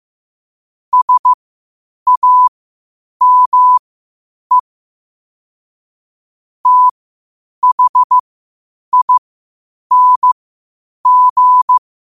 Morse code, heard as a keyed tone, about 15 wpm, spaced as for 8 wpm, 1000 Hz. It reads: SAME THING
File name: qotdfiles / 8wpm